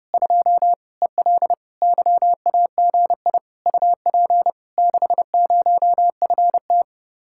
Morse code, heard as a keyed tone, about 31 wpm, 700 Hz. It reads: 2 EL YAGI UP 60FT